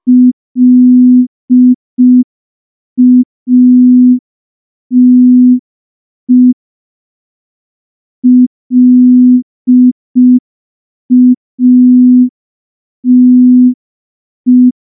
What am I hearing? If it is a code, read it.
Morse code, heard as a keyed tone, about 5 wpm, 250 Hz. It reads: LATE LATE
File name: morse / morse-training-05WPM